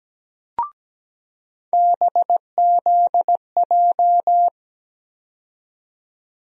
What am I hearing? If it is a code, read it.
Morse code, heard as a keyed tone, about 17 wpm, 700 Hz. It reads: BZJ
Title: USA Call Sign Postfixes - 3 Letters 17wpm